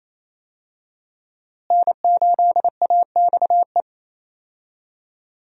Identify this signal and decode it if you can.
Morse code, heard as a keyed tone, about 28 wpm, 700 Hz. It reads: N8AXE
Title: USA Call Signs - 5 Characters - Repeated 3x - 28wpm